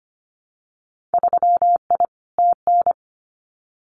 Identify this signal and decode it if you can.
Morse code, heard as a keyed tone, about 25 wpm, 700 Hz. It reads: 3I TN
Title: Winter Field Day Exchanges 25wpm